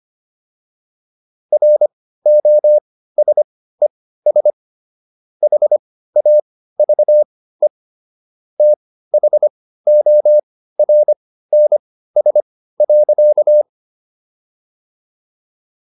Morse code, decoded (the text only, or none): ROSES HAVE THORNS.